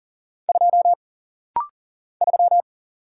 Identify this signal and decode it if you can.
Morse code, heard as a keyed tone, about 40 wpm, 700 Hz. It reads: 2 3
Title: Single Letter-Number - Rapid-Fire 40wpm